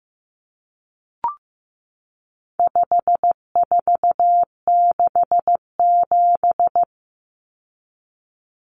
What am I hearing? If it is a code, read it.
Morse code, heard as a keyed tone, about 15 wpm, 700 Hz. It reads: 5467
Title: Common Cut Numbers - 4-Digit 15wpm